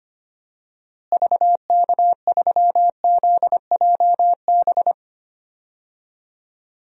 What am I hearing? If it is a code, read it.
Morse code, heard as a keyed tone, about 25 wpm, 700 Hz. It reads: VK3ZJB
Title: DXCC-100 Call Signs - 6 Characters - Repeated 2x - 25wpm